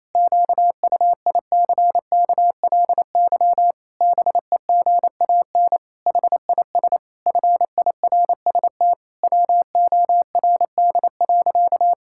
Morse code, heard as a keyed tone, about 28 wpm, 700 Hz. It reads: QUICKLY BEGAN HIS FIRST WORD.
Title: Pangram Challenge v2 - 28wpm